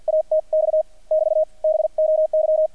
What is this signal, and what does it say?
Morse code, heard as a keyed tone, about 50 wpm, spaced as for 42 wpm, 630 Hz, with radio noise. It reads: ATX <BT> BOY
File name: archived / CRSnet